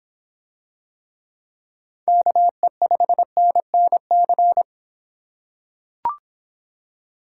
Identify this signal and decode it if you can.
Morse code, heard as a keyed tone, about 26 wpm, 700 Hz. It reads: KE5NNC